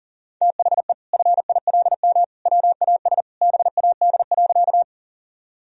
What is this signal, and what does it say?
Morse code, heard as a keyed tone, about 40 wpm, 700 Hz. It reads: THE FILM WAS BAD.
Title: Sentences from Top 1000 Words - Review Part 3 - 40wpm